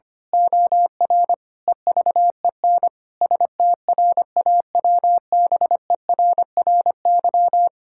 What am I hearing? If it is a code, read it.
Morse code, heard as a keyed tone, about 25 wpm, 700 Hz. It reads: OR EVEN STRAWBERRY
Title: Pangram Challenge v2 - 25wpm